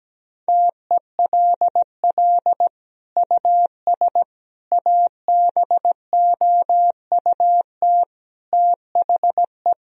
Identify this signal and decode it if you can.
Morse code, heard as a keyed tone, about 17 wpm, 700 Hz. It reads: TELL US ABOUT THE
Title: Sentences from Top 1000 Words - Review Part 4 - 17wpm